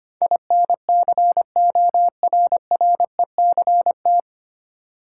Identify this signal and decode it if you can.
Morse code, heard as a keyed tone, about 25 wpm, 690 Hz. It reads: INCORRECT
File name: practice / 25wpm / 9 Letter Words - First 3 Letters Given - 25wpm